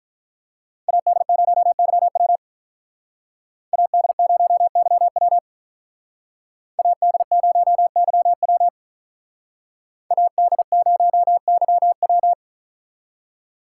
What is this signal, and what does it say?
Morse code, heard as a keyed tone, about 46 wpm, 700 Hz. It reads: AD0YW AD0YW AD0YW AD0YW